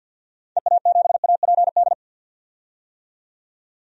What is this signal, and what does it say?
Morse code, heard as a keyed tone, about 50 wpm, 700 Hz. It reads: EA7APD
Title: DXCC-100 Call Signs - 6 Characters - 50wpm